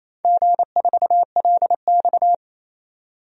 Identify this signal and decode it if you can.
Morse code, heard as a keyed tone, about 28 wpm, 700 Hz. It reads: G4LX
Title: DXCC-100 Call Signs - 4 Characters - 28wpm